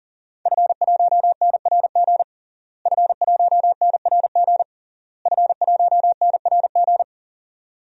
Morse code, heard as a keyed tone, about 40 wpm, 700 Hz. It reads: F1NRG F1NRG F1NRG
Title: DXCC-100 Call Signs - 5 Characters - Repeated 3x - 40wpm